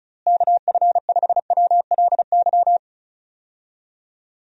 Morse code, heard as a keyed tone, about 34 wpm, 700 Hz. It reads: KF5WLY
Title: USA Call Signs - 6 Characters - Repeated 3x - 35wpm